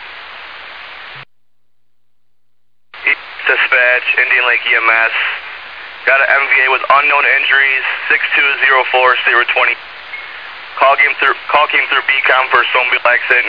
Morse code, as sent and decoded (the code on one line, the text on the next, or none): none